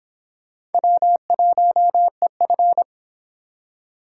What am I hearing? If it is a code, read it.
Morse code, heard as a keyed tone, about 30 wpm, 700 Hz. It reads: W1EF